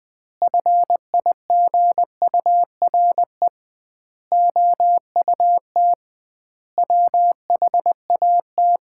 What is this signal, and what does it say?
Morse code, heard as a keyed tone, about 20 wpm, 700 Hz. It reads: FIGURE OUT WHAT
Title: 2x Word-Spacing - Sets of 5 Words - 20wpm